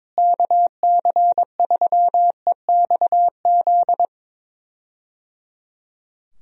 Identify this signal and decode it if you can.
Morse code, heard as a keyed tone, about 22 wpm, 700 Hz. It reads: KC3EXZ
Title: USA Call Signs - 6 Characters - Repeated 3x - 22wpm